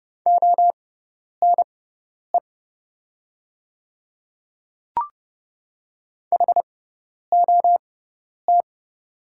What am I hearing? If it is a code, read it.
Morse code, heard as a keyed tone, about 31 wpm, spaced as for 10 wpm, 700 Hz. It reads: ONE HOT
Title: 6x Character Spacing - Top 100 Words - 30wpm